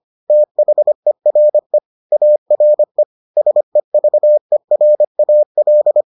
Morse code, heard as a keyed tone, about 25 wpm, 600 Hz. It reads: THERE ARE SEVERAL